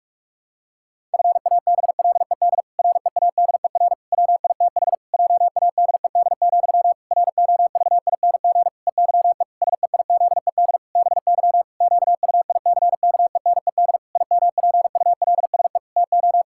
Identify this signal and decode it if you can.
Morse code, heard as a keyed tone, about 46 wpm, 700 Hz. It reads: FABLED READER WITH JADED, ROVING EYE SEIZED BY QUICKENED IMPULSE TO
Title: Pangram Challenge v2 - 45wpm